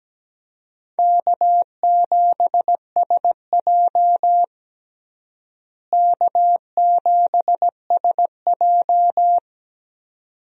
Morse code, heard as a keyed tone, about 17 wpm, 700 Hz. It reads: K7SJ K7SJ
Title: USA Call Signs - 4 Characters - Repeated 2x - 17wpm